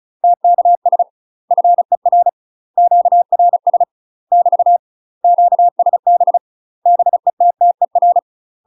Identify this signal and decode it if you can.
Morse code, heard as a keyed tone, about 34 wpm, 700 Hz. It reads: TKS FER QRS <BT> QSB BETTER
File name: UK MASTER QSOs TRAINING / 35WPM / UK MASTER QSOs 0034